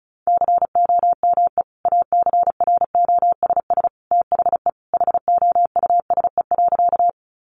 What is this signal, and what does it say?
Morse code, heard as a keyed tone, about 34 wpm, 700 Hz. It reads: COME ACROSS THE HOUSE.
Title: Sentences from Top 700 Words 35wpm